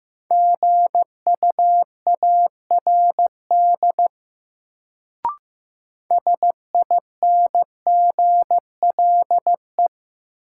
Morse code, heard as a keyed tone, about 15 wpm, 700 Hz. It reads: GUARD SINGLE
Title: Sentences from Top 700 Words - New Word Drill - 15wpm